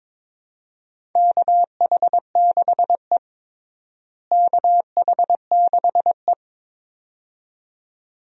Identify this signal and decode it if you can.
Morse code, heard as a keyed tone, about 22 wpm, 700 Hz. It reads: KH6E KH6E
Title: DXCC-100 Call Signs - 4 Characters - Repeated 2x - 22wpm